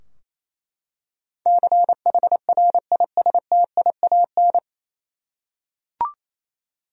Morse code, heard as a keyed tone, about 28 wpm, 700 Hz. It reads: CHRISTIAN
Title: USA Common Names - 28wpm